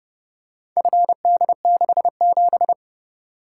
Morse code, heard as a keyed tone, about 29 wpm, 700 Hz. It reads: FD67